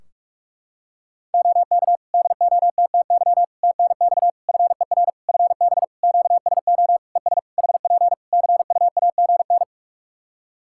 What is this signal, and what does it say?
Morse code, heard as a keyed tone, about 47 wpm, 700 Hz. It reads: OK DOTTY TNX FER FB QSO ES HP CUAGN
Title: K1USN SST QSO Practice 45wpm